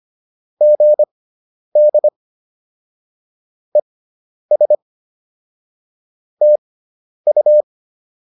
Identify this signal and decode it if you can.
Morse code, heard as a keyed tone, about 25 wpm, spaced as for 10 wpm, 600 Hz. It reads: GD ES TU
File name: QSO107 10